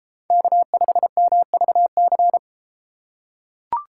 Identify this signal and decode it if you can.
Morse code, heard as a keyed tone, about 30 wpm, 690 Hz. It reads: K5MVC